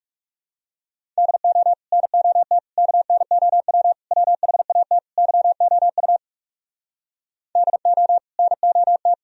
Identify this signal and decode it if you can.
Morse code, heard as a keyed tone, about 37 wpm, 700 Hz. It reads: DO NOT KNOW WHAT YOU DO NOT
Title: Speed-Racing - Sets of 5 Words 30wpm